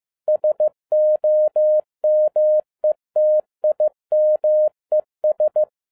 Morse code, heard as a keyed tone, about 15 wpm, 610 Hz. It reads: SOMETIMES